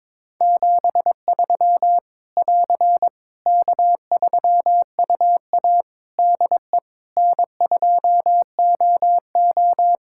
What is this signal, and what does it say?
Morse code, heard as a keyed tone, about 22 wpm, 700 Hz. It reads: 73 <AR> K3UA DE N2OO
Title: K1USN SST QSO Practice 22wpm